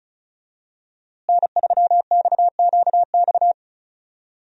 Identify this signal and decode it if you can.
Morse code, heard as a keyed tone, about 34 wpm, 700 Hz. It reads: N3XQX
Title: USA Call Signs - 5 Characters - Repeated 2x - 35wpm